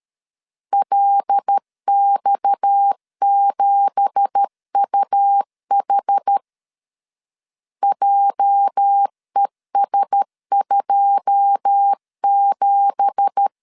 Morse code, heard as a keyed tone, about 13 wpm, 800 Hz. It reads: LX7UH JES27
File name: LEZIONI / 9 2867 12wpm 2322